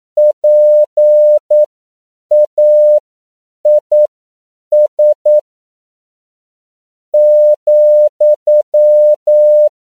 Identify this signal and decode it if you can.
Morse code, heard as a keyed tone, about 9 wpm, spaced as for 6 wpm, 600 Hz. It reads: PAIS,
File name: CW / 10PPM